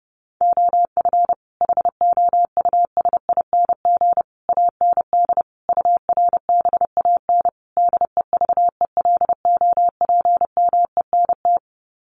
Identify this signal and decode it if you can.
Morse code, heard as a keyed tone, about 30 wpm, 700 Hz. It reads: OF HOUSING AND URBAN DEVELOPMENT
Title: Sets of 5 Words Encore2 30wpm